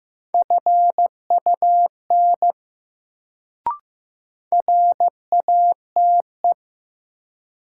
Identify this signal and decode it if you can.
Morse code, heard as a keyed tone, about 15 wpm, 700 Hz. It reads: FUN RATE
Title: Sentences from Top 700 Words - New Word Drill - 15wpm